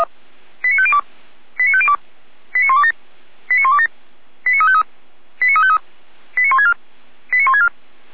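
Five-tone selective call (ZVEI), heard as quick runs of five tones, tones of about 70 ms. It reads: 89502 89502 89127 89127 89353 89353 89164 89164
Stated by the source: ZVEI five-tone alert received by SDR, from Feuerwehr Orlamünde's audio archive.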